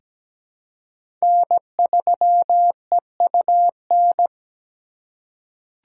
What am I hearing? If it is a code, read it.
Morse code, heard as a keyed tone, about 17 wpm, 700 Hz. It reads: N3EUN
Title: USA Call Signs - 5 Characters - 8x Character Spacing - 17wpm